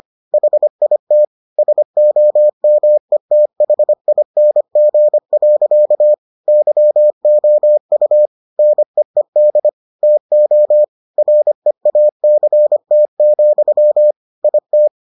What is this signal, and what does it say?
Morse code, heard as a keyed tone, about 25 wpm, 600 Hz. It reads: HIT SOMETHING. YOU NEED TO REACT, IT